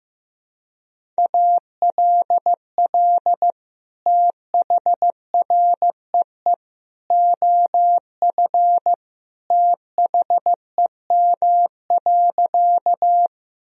Morse code, heard as a keyed tone, about 15 wpm, 700 Hz. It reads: ALL THREE OF THEM.
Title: Sentences from Top 1000 Words - Review Part 1 - 15wpm